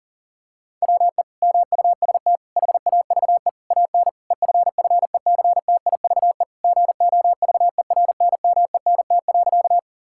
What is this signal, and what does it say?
Morse code, heard as a keyed tone, about 37 wpm, 690 Hz. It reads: WE MUST HAVE AN EFFECTIVE GOVERNMENT.
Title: Sentences from Top 1000 Words - Review Part 6 - 40wpm